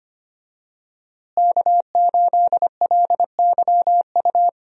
Morse code, heard as a keyed tone, about 25 wpm, 700 Hz. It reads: K8LYU
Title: USA Call Signs - 5 Characters - Repeated 3x - 25wpm